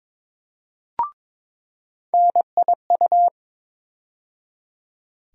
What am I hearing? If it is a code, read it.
Morse code, heard as a keyed tone, about 21 wpm, 700 Hz. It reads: NIU
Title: USA Call Sign Postfixes - 3 Letters 22wpm